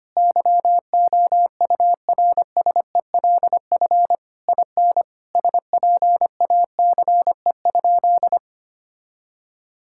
Morse code, heard as a keyed tone, about 25 wpm, 700 Hz. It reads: YOURSELF IN SPACE?
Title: Sentences from Top 1000 Words - Review Part 5 - 25wpm